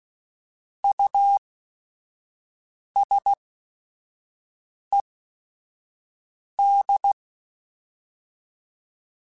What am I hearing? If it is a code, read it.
Morse code, heard as a keyed tone, about 16 wpm, spaced as for 5 wpm, 800 Hz. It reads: USED
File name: Technical Subjects / Technical Subjects-mp3s / Reactance @5